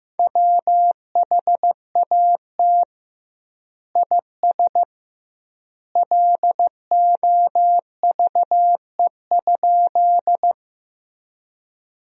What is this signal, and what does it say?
Morse code, heard as a keyed tone, about 15 wpm, 700 Hz. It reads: WHAT IS LOVE?
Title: 2x Word-Spacing - Sentences from Top 400 Words - 15wpm